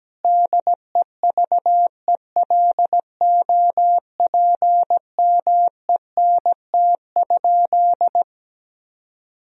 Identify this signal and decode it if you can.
Morse code, heard as a keyed tone, about 17 wpm, 700 Hz. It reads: DEVELOPMENT?
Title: Sentences from Top 1000 Words - Review Part 4 - 17wpm